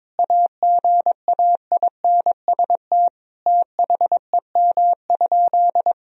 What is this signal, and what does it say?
Morse code, heard as a keyed tone, about 22 wpm, 700 Hz. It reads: AGAINST THEM?
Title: Sentences from Top 1000 Words - Review Part 2 - 22wpm